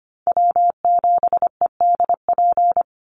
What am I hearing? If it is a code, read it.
Morse code, heard as a keyed tone, about 25 wpm, 700 Hz. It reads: W7EDP
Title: USA Call Signs 25wpm